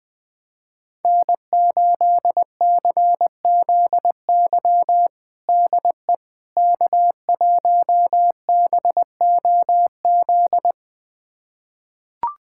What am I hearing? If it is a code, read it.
Morse code, heard as a keyed tone, about 20 wpm, 700 Hz. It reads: N8CZY DE K1BOZ
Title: USA Callsign DE Callsign - 5 Characters - 20wpm